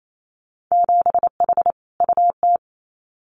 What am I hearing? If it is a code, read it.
Morse code, heard as a keyed tone, about 28 wpm, 700 Hz. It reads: 7H UT
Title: Winter Field Day Exchanges Encore 28wpm